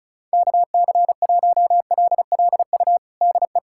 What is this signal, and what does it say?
Morse code, heard as a keyed tone, about 36 wpm, 700 Hz. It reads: KC1LLU DE
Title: USA Callsign DE Callsign 35wpm